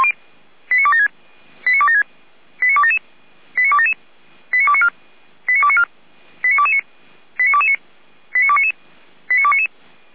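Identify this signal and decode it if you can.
Five-tone selective call (ZVEI), heard as quick runs of five tones, tones of about 70 ms. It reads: 89276 89276 89277 89277 89294 89294 89229 89229 89299 89299
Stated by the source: ZVEI five-tone selective call received via SDR.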